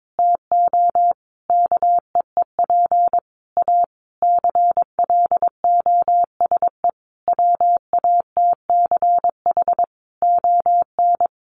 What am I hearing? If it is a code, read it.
Morse code, heard as a keyed tone, about 22 wpm, 700 Hz. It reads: TO KEEP A CLOSE WATCH ON
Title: Sentences from Top 1300 Words 22wpm